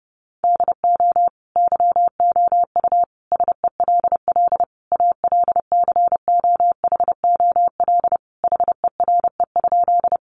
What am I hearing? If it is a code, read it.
Morse code, heard as a keyed tone, about 28 wpm, 700 Hz. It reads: DO YOU SELL ALCOHOL HERE?
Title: Sentences from Top 1400 Words 30wpm